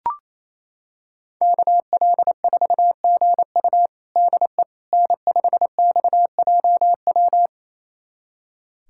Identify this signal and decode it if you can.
Morse code, heard as a keyed tone, about 28 wpm, 700 Hz. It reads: KL4GU DE N5XJW
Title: USA Callsign DE Callsign - 5 Characters - 28wpm